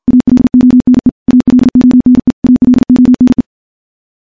Morse code, heard as a keyed tone, about 51 wpm, 250 Hz. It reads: ALONE ALONE ALONE
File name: morse / morse-training-50WPM